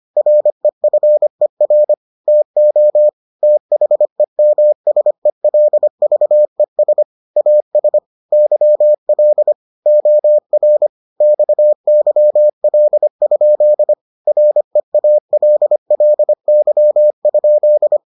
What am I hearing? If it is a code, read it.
Morse code, heard as a keyed tone, about 25 wpm, 600 Hz. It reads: REFER TO THEMSELVES AS YL OR XYL? REALLY?